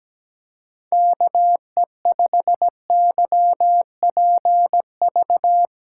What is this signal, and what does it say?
Morse code, heard as a keyed tone, about 17 wpm, 700 Hz. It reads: KE5YPV
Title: USA Call Signs - 6 Characters - 8x Character Spacing - 17wpm